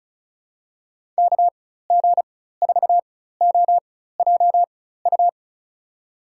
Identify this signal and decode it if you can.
Morse code, heard as a keyed tone, about 34 wpm, spaced as for 16 wpm, 700 Hz. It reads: KG4OJU
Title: USA Call Signs - 6 Characters - 4x Character Spacing - 35wpm